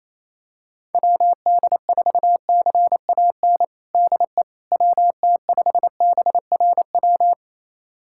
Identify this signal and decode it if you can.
Morse code, heard as a keyed tone, about 28 wpm, 700 Hz. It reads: WD4CAN DE WT5BRW
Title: USA Callsign DE Callsign 28wpm